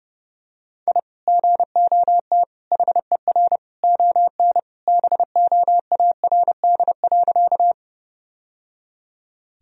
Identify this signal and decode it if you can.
Morse code, heard as a keyed tone, about 30 wpm, 700 Hz. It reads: I GOT HER ON BOARD.
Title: Sentences from Top 1000 Words - Review Part 4 - 30wpm